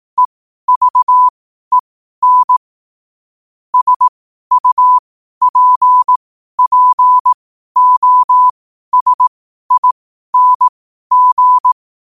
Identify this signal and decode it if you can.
Morse code, heard as a keyed tone, about 18 wpm, spaced as for 12 wpm, 1000 Hz. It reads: EVEN SUPPOSING
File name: qotdfiles / 12wpm